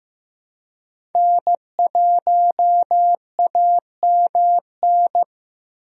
Morse code, heard as a keyed tone, about 15 wpm, 700 Hz. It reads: N1AMN
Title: USA Call Signs - 5 Characters - Repeated 3x - 15wpm